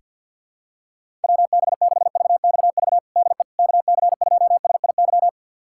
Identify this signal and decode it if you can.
Morse code, heard as a keyed tone, about 49 wpm, 700 Hz. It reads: WB6VXV DE KC2SIY
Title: USA Callsign DE Callsign 50wpm